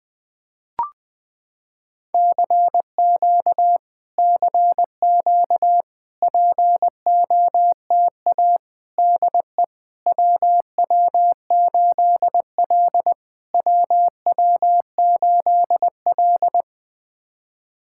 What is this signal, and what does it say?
Morse code, heard as a keyed tone, about 20 wpm, 700 Hz. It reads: CQ CQ POTA DE WW8L WW8L
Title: POTA QSO Practice - 20wpm